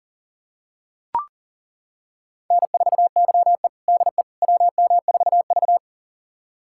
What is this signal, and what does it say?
Morse code, heard as a keyed tone, about 40 wpm, 700 Hz. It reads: N4YE DE WM4V
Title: USA Callsign DE Callsign - 4 Characters - 40wpm